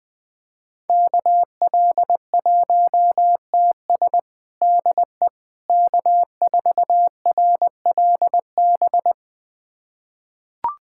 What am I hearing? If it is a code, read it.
Morse code, heard as a keyed tone, about 20 wpm, 700 Hz. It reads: KL1TS DE K4RLB
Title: USA Callsign DE Callsign 20wpm